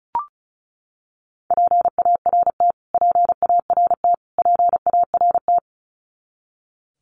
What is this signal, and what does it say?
Morse code, heard as a keyed tone, about 34 wpm, 700 Hz. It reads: PART PART PART
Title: Top 300 Words Repeated 35wpm